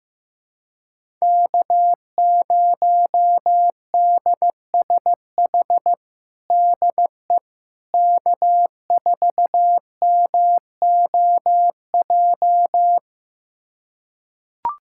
Morse code, heard as a keyed tone, about 15 wpm, 700 Hz. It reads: K0DSH DE K4MOJ
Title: USA Callsign DE Callsign - 5 Characters - 15wpm